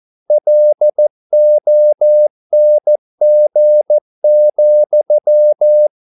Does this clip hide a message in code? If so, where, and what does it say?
Yes: Morse code from 0.3 to 5.9 s, heard as a keyed tone, about 14 wpm, 600 Hz. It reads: LONG,